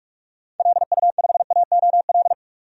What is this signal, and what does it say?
Morse code, heard as a keyed tone, about 43 wpm, 700 Hz. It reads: LU5AOL
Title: DXCC-100 Call Signs - 6 Characters - 45wpm